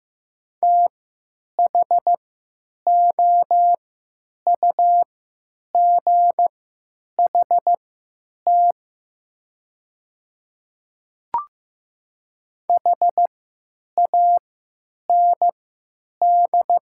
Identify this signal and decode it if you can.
Morse code, heard as a keyed tone, about 15 wpm, spaced as for 8 wpm, 700 Hz. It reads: THOUGHT HAND
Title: 3x Character Spacing - Top 500 Words - 15wpm